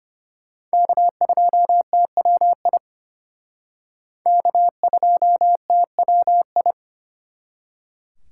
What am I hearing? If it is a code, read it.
Morse code, heard as a keyed tone, about 32 wpm, 700 Hz. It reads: K2TWI K2TWI